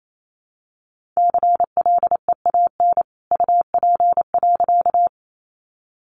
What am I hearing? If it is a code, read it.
Morse code, heard as a keyed tone, about 28 wpm, 700 Hz. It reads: CLEAN UP.
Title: A Sound Of Thunder 28wpm